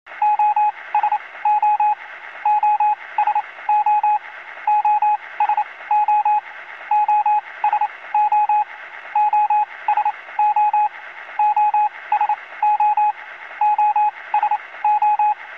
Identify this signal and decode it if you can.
Morse code, heard as a keyed tone, about 28 wpm, spaced as for 20 wpm, 840 Hz, with radio noise. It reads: OSO OSO OSO OSO OSO OSO OSO
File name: zvuki-morze-sos